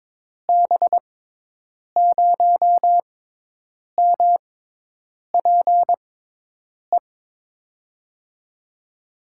Morse code, heard as a keyed tone, about 22 wpm, spaced as for 8 wpm, 700 Hz. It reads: B0MPE
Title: USA Call Signs - 6 Characters - 6x Character Spacing - 22wpm